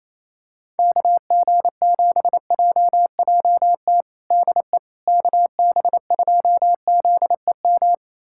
Morse code, heard as a keyed tone, about 27 wpm, 700 Hz. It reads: KG7JJT DE KB2ZEM
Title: USA Callsign DE Callsign - 6 Characters - 28wpm